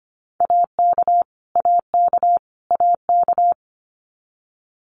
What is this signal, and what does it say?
Morse code, heard as a keyed tone, about 25 wpm, 700 Hz. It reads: AK AK AK
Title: US States Repeated 3x 25wpm